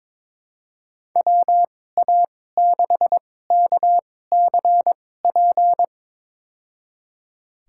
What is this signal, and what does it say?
Morse code, heard as a keyed tone, about 21 wpm, spaced as for 16 wpm, 700 Hz. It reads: WA6KCP
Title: USA Call Signs - 6 Characters - 2x Character Spacing - 22wpm